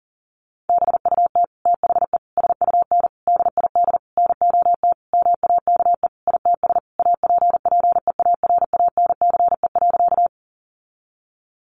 Morse code, heard as a keyed tone, about 40 wpm, 700 Hz. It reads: BUT THE SUN DID NOT MAKE ITS APPEARANCE.
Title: Sentences from Top 1100 Words 40wpm